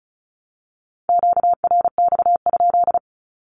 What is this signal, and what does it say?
Morse code, heard as a keyed tone, about 34 wpm, 700 Hz. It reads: QRX?
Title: Q-Codes Repeated 3x 35wpm